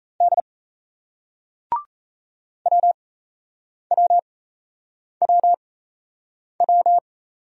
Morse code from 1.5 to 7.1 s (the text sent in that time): WWWW